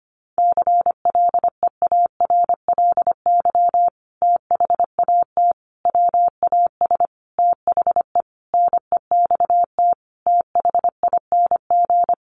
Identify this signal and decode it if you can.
Morse code, heard as a keyed tone, about 25 wpm, 700 Hz. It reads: CLEARLY THAT WAS THE NEXT THING
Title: Sentences from Top 1300 Words 25wpm